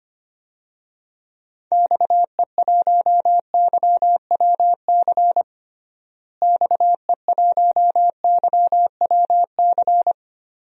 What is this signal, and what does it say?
Morse code, heard as a keyed tone, about 25 wpm, 700 Hz. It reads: XE1YWC XE1YWC
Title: DXCC-100 Call Signs - 6 Characters - Repeated 3x - 25wpm